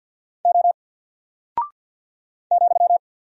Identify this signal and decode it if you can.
Morse code, heard as a keyed tone, about 48 wpm, 700 Hz. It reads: O ,
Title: Single Character (Letter:Number:Punctuation:Prosign) Alt 50wpm